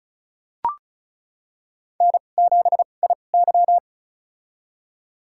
Morse code, heard as a keyed tone, about 36 wpm, spaced as for 25 wpm, 700 Hz. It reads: N7IY
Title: USA Call Signs - 2x Character Spacing - 35wpm